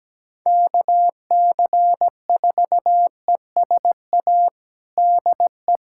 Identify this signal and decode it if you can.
Morse code, heard as a keyed tone, about 17 wpm, 700 Hz. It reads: KC4ESA DE
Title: USA Callsign DE Callsign - 6 Characters - 17wpm